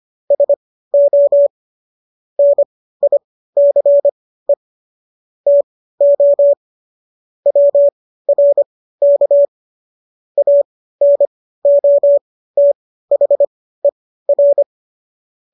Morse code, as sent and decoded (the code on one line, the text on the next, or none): ... --- / -. .. -.-. / - --- / .-- .-. -.- / .- -. --- - .... .-.
SO NICE TO WRK ANOTHER